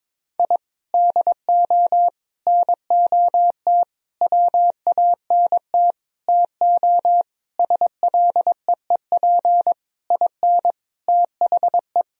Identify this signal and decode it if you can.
Morse code, heard as a keyed tone, about 22 wpm, 700 Hz. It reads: I DO NOT WANT TO SLEEP IN THE
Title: Sentences from Top 1000 Words - Review Part 2 - 22wpm